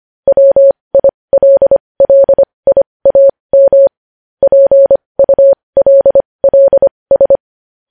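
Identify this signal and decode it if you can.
Morse code, heard as a keyed tone, about 25 wpm, spaced as for 20 wpm, 550 Hz. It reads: WILLIAM PULLS
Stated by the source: Morse code news headlines sent at 20 words per minute.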